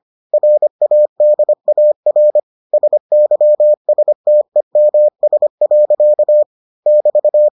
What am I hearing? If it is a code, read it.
Morse code, heard as a keyed tone, about 25 wpm, 600 Hz. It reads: RADAR SYSTEMS. <BT>